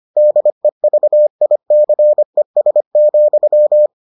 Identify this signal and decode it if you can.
Morse code, heard as a keyed tone, about 25 wpm, 600 Hz. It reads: DEVICES,